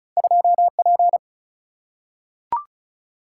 Morse code, heard as a keyed tone, about 34 wpm, 700 Hz. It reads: 2P
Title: Single-Letter Single-Number 35wpm